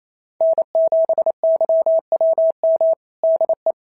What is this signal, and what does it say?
Morse code, heard as a keyed tone, about 25 wpm, 650 Hz. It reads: N7YWM DE
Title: SOTA QSO Practice 28wpm